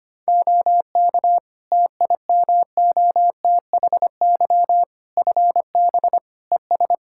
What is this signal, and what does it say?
Morse code, heard as a keyed tone, about 25 wpm, 700 Hz. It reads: OK TIMOTHY FB ES
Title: Practice QSO - Exchange 2 - Info - 25wpm